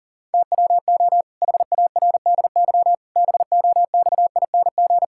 Morse code, heard as a keyed tone, about 37 wpm, 700 Hz. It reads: TWO HARDY BOXING
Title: Pangram Challenge v2 - 40wpm